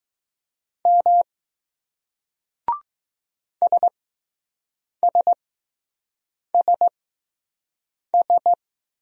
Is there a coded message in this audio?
Yes: Morse code from 0.9 to 8.5 s, heard as a keyed tone, about 20 wpm, spaced as for 6 wpm, 700 Hz. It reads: M SSSS